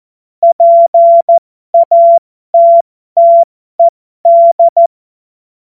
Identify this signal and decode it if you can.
Morse code, heard as a keyed tone, about 14 wpm, spaced as for 12 wpm, 680 Hz. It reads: PATTED